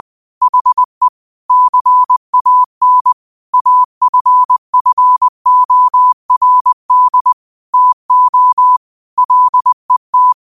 Morse code, heard as a keyed tone, about 20 wpm, 1000 Hz. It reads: HE CAN AFFORD TO LET
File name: qotdfiles / 20wpm